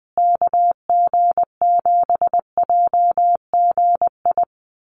Morse code, heard as a keyed tone, about 20 wpm, 700 Hz. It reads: KG7JGI
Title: USA Call Signs Repeated 3x 20wpm